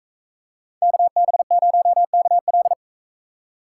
Morse code, heard as a keyed tone, about 41 wpm, 700 Hz. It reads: KB0KL